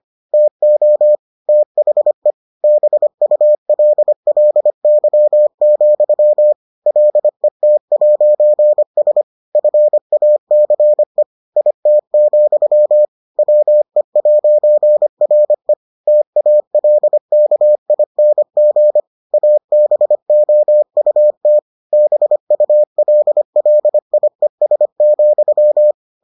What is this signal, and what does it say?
Morse code, heard as a keyed tone, about 25 wpm, 600 Hz. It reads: TO THE BULLY, LET'S FACE IT, WE'RE TALKING ABOUT BULLIES,